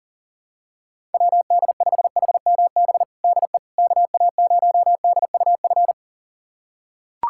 Morse code, heard as a keyed tone, about 40 wpm, 700 Hz. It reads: WD5HMB DE KA0DUF
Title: USA Callsign DE Callsign 40wpm